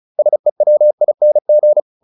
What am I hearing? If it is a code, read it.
Morse code, heard as a keyed tone, about 34 wpm, 600 Hz. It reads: SEWING